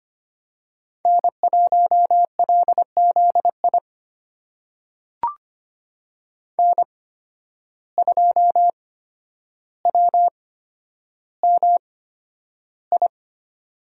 Morse code, heard as a keyed tone, about 25 wpm, spaced as for 11 wpm, 700 Hz. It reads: N1LZI N2WMI
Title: USA Call Signs - 5 Characters - 8x Character Spacing - 25wpm